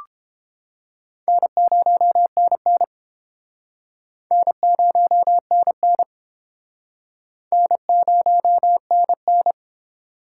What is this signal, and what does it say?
Morse code, heard as a keyed tone, about 30 wpm, 700 Hz. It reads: N0NN N0NN N0NN